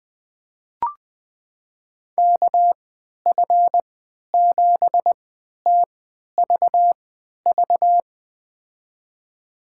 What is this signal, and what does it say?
Morse code, heard as a keyed tone, about 20 wpm, spaced as for 11 wpm, 700 Hz. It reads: KF7TVV
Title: USA Call Signs - 3x Character Spacing - 20wpm